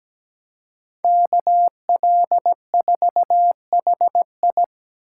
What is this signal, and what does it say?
Morse code, heard as a keyed tone, about 17 wpm, 700 Hz. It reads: KL4HI
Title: USA Call Signs - 5 Characters - 4x Character Spacing - 17wpm